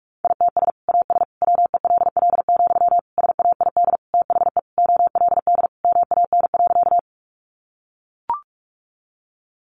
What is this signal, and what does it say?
Morse code, heard as a keyed tone, about 44 wpm, 700 Hz. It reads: ITS AS WELL, SAID THE OLD MAN.
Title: Sentences from Top 600 Words 45wpm